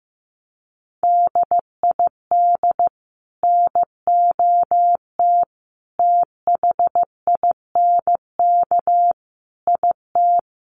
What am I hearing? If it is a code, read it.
Morse code, heard as a keyed tone, about 15 wpm, 700 Hz. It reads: DID NOT THINK IT
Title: Sets of 5 Words Encore 15wpm